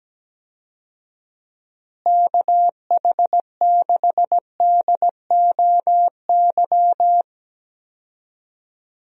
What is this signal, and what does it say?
Morse code, heard as a keyed tone, about 17 wpm, 700 Hz. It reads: KH6DOY
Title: DXCC-100 Call Signs - 6 Characters - Repeated 2x - 17wpm